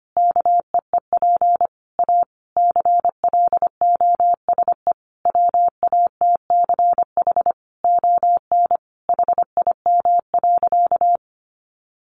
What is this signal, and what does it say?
Morse code, heard as a keyed tone, about 25 wpm, 700 Hz. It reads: KEEP A CLOSE WATCH ON HIM.
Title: Sentences from Top 1300 Words 25wpm